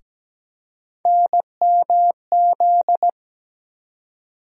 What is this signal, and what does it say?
Morse code, heard as a keyed tone, about 17 wpm, 700 Hz. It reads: NMZ
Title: USA Call Sign Postfixes - 3 Letters 17wpm